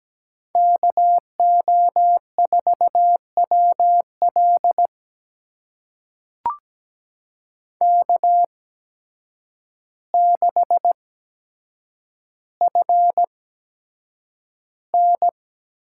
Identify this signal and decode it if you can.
Morse code, heard as a keyed tone, about 17 wpm, spaced as for 8 wpm, 700 Hz. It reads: KO4WL K6FN
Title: USA Call Signs - 5 Characters - 8x Character Spacing - 17wpm